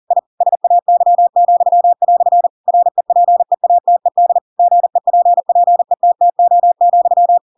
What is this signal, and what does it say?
Morse code, heard as a keyed tone, about 40 wpm, 700 Hz. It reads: I SAY," REPEATED GEPPETTO,
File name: Pinocchio / Pinocchio ch7 40 40 ik1hge